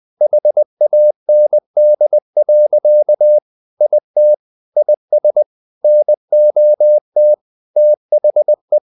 Morse code, heard as a keyed tone, about 20 wpm, 600 Hz. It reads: HAND. IT IS NOT THE
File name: the old man and the sea / the old man and the sea part 1 20wpm